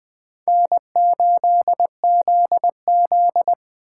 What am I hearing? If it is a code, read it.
Morse code, heard as a keyed tone, about 20 wpm, 700 Hz. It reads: N8ZZ